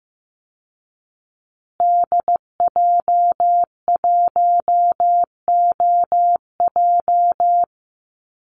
Morse code, heard as a keyed tone, about 14 wpm, 700 Hz. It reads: DJ1OJ
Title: Call Signs Repeated 2x - 15wpm